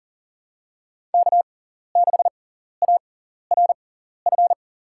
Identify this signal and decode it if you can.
Morse code, heard as a keyed tone, about 40 wpm, spaced as for 14 wpm, 700 Hz. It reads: K6ARF
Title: USA Call Signs - 5 Characters - 6x Character Spacing - 40wpm